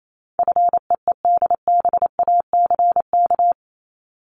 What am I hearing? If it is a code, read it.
Morse code, heard as a keyed tone, about 28 wpm, 700 Hz. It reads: FEEDBACK
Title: Top 1501-1750 Words 28wpm